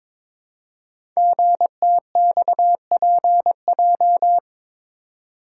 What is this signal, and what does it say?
Morse code, heard as a keyed tone, about 23 wpm, 700 Hz. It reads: GTXPJ
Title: Code Groups - 5 Letters - 22wpm